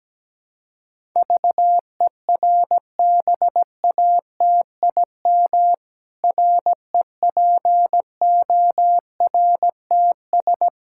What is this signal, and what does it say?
Morse code, heard as a keyed tone, about 17 wpm, 700 Hz. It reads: VERBATIM REPORTS
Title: Pangram Challenge v2 - 17wpm